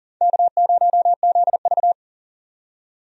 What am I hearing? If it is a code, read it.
Morse code, heard as a keyed tone, about 40 wpm, 700 Hz. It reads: K0ZV
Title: USA Call Signs - 4 Characters - 40wpm